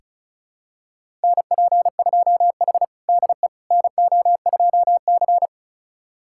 Morse code, heard as a keyed tone, about 36 wpm, 700 Hz. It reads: NP2H DE NO2C
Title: USA Callsign DE Callsign - 4 Characters - 35wpm